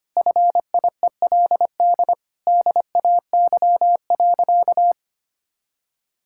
Morse code, heard as a keyed tone, about 25 wpm, 700 Hz. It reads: FIELD DAY.
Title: Sentences from Top 1000 Words - Review Part 6 - 25wpm